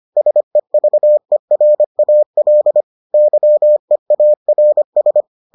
Morse code, heard as a keyed tone, about 25 wpm, 600 Hz. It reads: SEVERAL YEARS